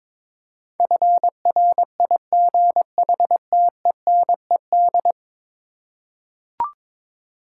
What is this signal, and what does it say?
Morse code, heard as a keyed tone, about 21 wpm, 700 Hz. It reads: FRIGHTENED
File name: practice / 22wpm / 10 Letter Words - First 3 Letters Given - 22wpm